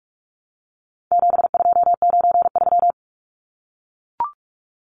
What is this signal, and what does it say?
Morse code, heard as a keyed tone, about 44 wpm, 700 Hz. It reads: 7293
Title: Numbers - 4-Digit 45wpm